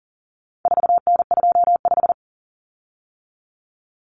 Morse code, heard as a keyed tone, about 40 wpm, 700 Hz. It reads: 4N25
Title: Common Cut Numbers - 4-Digit 40wpm